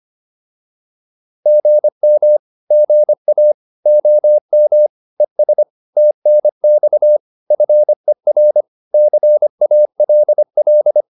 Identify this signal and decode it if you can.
Morse code, heard as a keyed tone, about 25 wpm, 600 Hz. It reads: GM GA OM ES TNX FER CALL
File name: QSO407 25